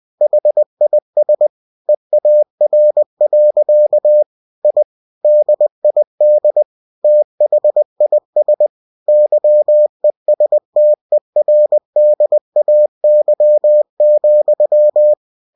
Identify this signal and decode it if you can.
Morse code, heard as a keyed tone, about 20 wpm, 600 Hz. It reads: HIS EAR. I DID THIS YESTERDAY,
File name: cw-books / winnie-the-pooh / winnie-20wpm-page-0054